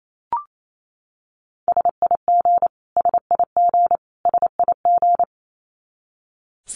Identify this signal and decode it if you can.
Morse code, heard as a keyed tone, about 28 wpm, 700 Hz. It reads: SIG SIG SIG
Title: Abbreviations Repeated 3x - 28wpm